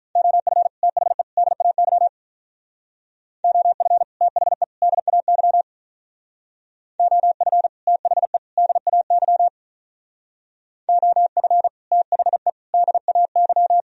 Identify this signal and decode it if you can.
Morse code, heard as a keyed tone, about 43 wpm, 700 Hz. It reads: OF THE DAY OF THE DAY OF THE DAY OF THE DAY
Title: Speed-Racing - Sets of 3 Words 35wpm